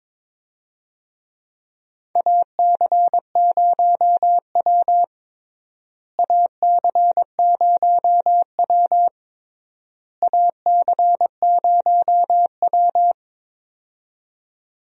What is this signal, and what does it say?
Morse code, heard as a keyed tone, about 22 wpm, 700 Hz. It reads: AC0W AC0W AC0W
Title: USA Call Signs - 4 Characters - Repeated 3x - 22wpm